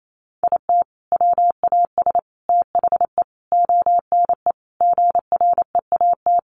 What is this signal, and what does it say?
Morse code, heard as a keyed tone, about 28 wpm, 700 Hz. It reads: IT WAS THE ONE GREAT
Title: Sentences from Top 1400 Words 28wpm